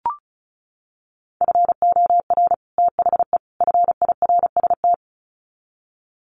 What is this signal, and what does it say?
Morse code, heard as a keyed tone, about 34 wpm, 700 Hz. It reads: FOR THE FIRST
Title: Sets of 3 Words 35wpm